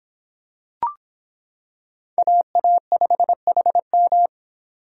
Morse code, heard as a keyed tone, about 26 wpm, 700 Hz. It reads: AA5HM